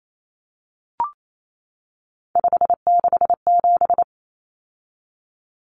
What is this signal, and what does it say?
Morse code, heard as a keyed tone, about 28 wpm, 700 Hz. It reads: 567
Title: Numbers - 3-Digit 28wpm